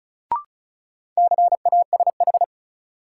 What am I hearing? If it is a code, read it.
Morse code, heard as a keyed tone, about 36 wpm, 700 Hz. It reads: CASH
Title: Sentences from Top 700 Words - New Word Drill - 35wpm